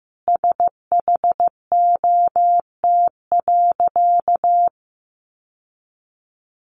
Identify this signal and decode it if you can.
Morse code, heard as a keyed tone, about 15 wpm, 700 Hz. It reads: SHOT.
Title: Sentences from Top 1100 Words 15wpm